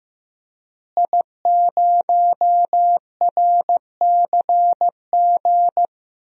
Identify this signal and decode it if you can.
Morse code, heard as a keyed tone, about 16 wpm, 700 Hz. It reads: I0RCG
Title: DXCC-100 Call Signs - 5 Characters - Repeated 2x - 15wpm